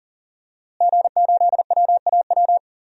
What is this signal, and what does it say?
Morse code, heard as a keyed tone, about 37 wpm, 700 Hz. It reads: G8WAW
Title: DXCC-100 Call Signs - 5 Characters - Repeated 3x - 40wpm